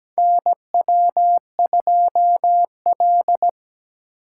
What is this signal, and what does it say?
Morse code, heard as a keyed tone, about 17 wpm, 700 Hz. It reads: NW2L
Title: USA Call Signs - 4 Characters - 4x Character Spacing - 17wpm